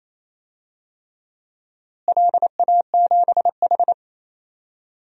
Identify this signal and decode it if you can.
Morse code, heard as a keyed tone, about 28 wpm, 700 Hz. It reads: LA7H
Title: DXCC-100 Call Signs - 4 Characters - Repeated 3x - 28wpm